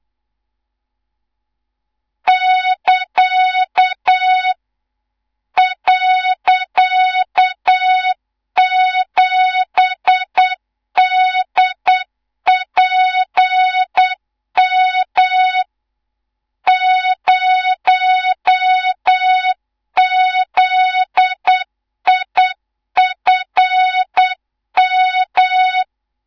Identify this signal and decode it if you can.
Morse code, heard as a keyed tone, about 8 wpm, 3700 Hz, with noise. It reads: <KA> .7DPM 0ZIFM
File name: Morze-HA2MN / hang / 18 IRASJELEK2